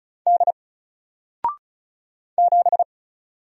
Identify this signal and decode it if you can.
Morse code, heard as a keyed tone, about 36 wpm, 700 Hz. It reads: D 7
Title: Single Character (Letter:Number:Punctuation:Prosign) Alt 35wpm